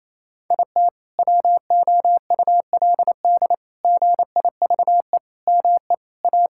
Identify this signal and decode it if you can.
Morse code, heard as a keyed tone, about 28 wpm, 700 Hz. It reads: IT WOULD GIVE ME A
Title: Sentences from Top 1600 Words 28wpm